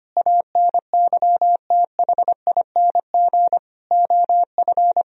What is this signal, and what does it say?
Morse code, heard as a keyed tone, about 25 wpm, 700 Hz. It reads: ANYTHING OF
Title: Sentences from Top 1000 Words - Review Part 3 - 25wpm